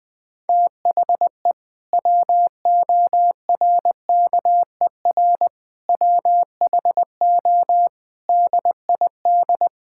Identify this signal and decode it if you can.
Morse code, heard as a keyed tone, about 21 wpm, 700 Hz. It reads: THE WORKER WHO DID
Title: Sentences with 7-Word Phrases - 20wpm